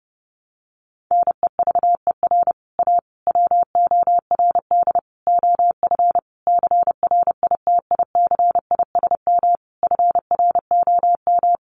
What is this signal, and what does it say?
Morse code, heard as a keyed tone, about 30 wpm, 700 Hz. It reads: NEVER A WORD OF CRITICISM FROM
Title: Sentences from Top 1300 Words 30wpm